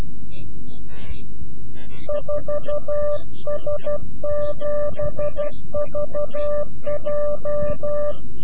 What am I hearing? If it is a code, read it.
Morse code, heard as a keyed tone, about 12 wpm, 590 Hz. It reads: 4S7VJ
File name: CW-QSO / CQ Call / CQ-QSO